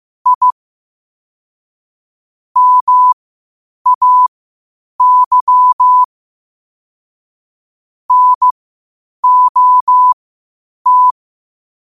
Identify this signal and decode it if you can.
Morse code, heard as a keyed tone, about 15 wpm, spaced as for 8 wpm, 1000 Hz. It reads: I MAY NOT
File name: qotdfiles / 8wpm